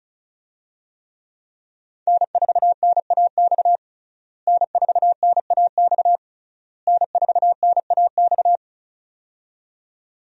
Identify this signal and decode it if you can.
Morse code, heard as a keyed tone, about 36 wpm, 700 Hz. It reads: N4NAX N4NAX N4NAX
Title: USA Call Signs - 5 Characters - Repeated 3x - 35wpm